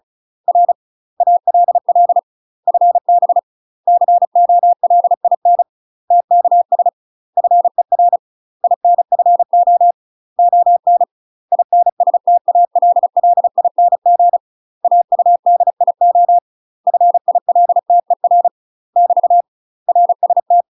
Morse code, heard as a keyed tone, about 34 wpm, 700 Hz. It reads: R ALL FB COLIN TKS FER INFO ON INSTALLING AUDIO FILTER <BT> RST
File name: UK MASTER QSOs TRAINING / 35WPM / UK MASTER QSOs 0037